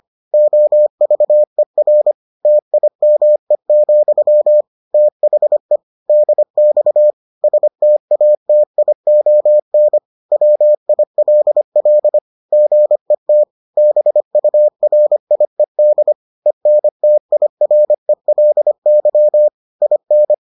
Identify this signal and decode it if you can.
Morse code, heard as a keyed tone, about 25 wpm, 600 Hz. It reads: OVER TIME, THE DX STATION WILL GET BURIED ENTIRELY IN